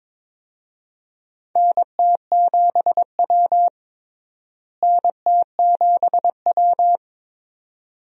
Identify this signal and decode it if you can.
Morse code, heard as a keyed tone, about 22 wpm, 700 Hz. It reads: NT7W NT7W
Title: USA Call Signs - 4 Characters - Repeated 2x - 22wpm